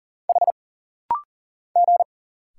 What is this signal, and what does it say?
Morse code, heard as a keyed tone, about 40 wpm, 700 Hz. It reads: H G
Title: Single Letter-Number - Rapid-Fire 40wpm